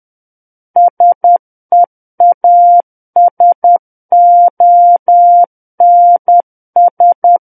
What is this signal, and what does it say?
Morse code, heard as a keyed tone, about 10 wpm, 700 Hz. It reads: SEASONS
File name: Book One / 01-Chapter 1 - 10 WPM